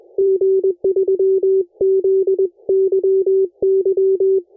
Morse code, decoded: G3ZYY